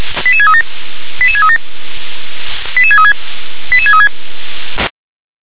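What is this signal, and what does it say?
Five-tone selective call (ZVEI), heard as quick runs of five tones, tones of about 70 ms. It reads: 88637 88637 88636 88636
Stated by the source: Received on SDR.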